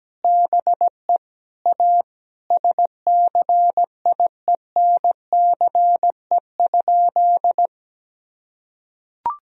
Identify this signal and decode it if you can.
Morse code, heard as a keyed tone, about 17 wpm, 700 Hz. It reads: BE A SCIENCE?
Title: Sentences from Top 1000 Words - Review Part 2 - 17wpm